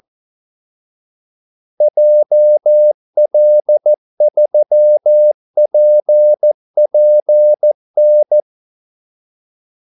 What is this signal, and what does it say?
Morse code, heard as a keyed tone, about 14 wpm, 600 Hz. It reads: JL3PPN
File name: Callsigns-14wpm